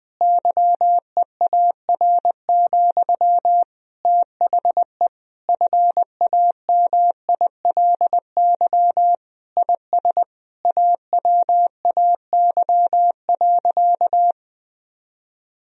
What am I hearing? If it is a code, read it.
Morse code, heard as a keyed tone, about 21 wpm, 700 Hz. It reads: YEAR, THE FAMILY IS AWAY.
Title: Sentences from Top 1000 Words - Review Part 2 - 20wpm